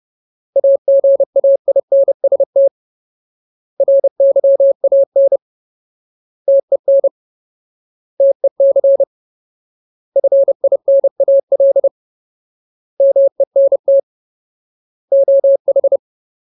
Morse code, heard as a keyed tone, about 30 wpm, 550 Hz. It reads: AGAINST RYAN TEN TEC FINAL MENT OH
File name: Ultimate Word List / 30WPM Ultimate